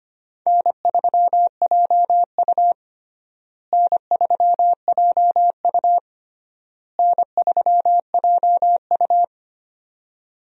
Morse code, heard as a keyed tone, about 25 wpm, 700 Hz. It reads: N3JU N3JU N3JU
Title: USA Call Signs - 4 Characters - Repeated 3x - 25wpm